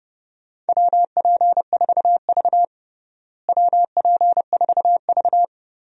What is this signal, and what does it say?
Morse code, heard as a keyed tone, about 29 wpm, 700 Hz. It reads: WP4V WP4V
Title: USA Call Signs - 4 Characters - Repeated 3x - 30wpm